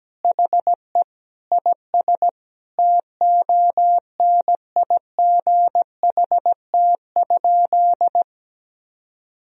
Morse code, heard as a keyed tone, about 17 wpm, 700 Hz. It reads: HE IS TONIGHT?
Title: Sentences from Top 1000 Words - Review Part 3 - 17wpm